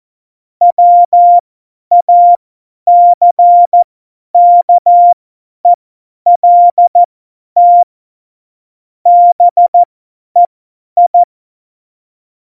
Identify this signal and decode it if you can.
Morse code, heard as a keyed tone, about 14 wpm, spaced as for 10 wpm, 700 Hz. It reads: WACKELT BEI
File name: Morsekurs / Witze / Witz 14 Juli 2020, 50 F70